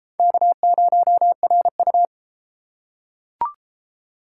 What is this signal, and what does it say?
Morse code, heard as a keyed tone, about 32 wpm, 700 Hz. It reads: K0RU